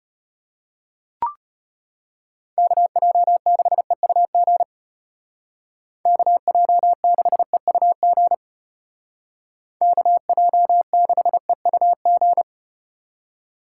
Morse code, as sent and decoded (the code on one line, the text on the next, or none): -.- .--- -.... ..- --. / -.- .--- -.... ..- --. / -.- .--- -.... ..- --.
KJ6EUG KJ6EUG KJ6EUG